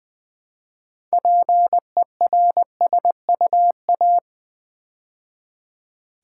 Morse code, heard as a keyed tone, about 20 wpm, 700 Hz. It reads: PERSUA
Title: Build A Word - 2 Letters At A Time - 8-Letter Words - 20wpm